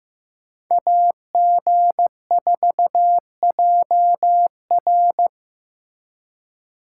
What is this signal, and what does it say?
Morse code, heard as a keyed tone, about 18 wpm, 700 Hz. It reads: AG4JR
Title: Speed Racing - USA 2x2 Call Signs 15wpm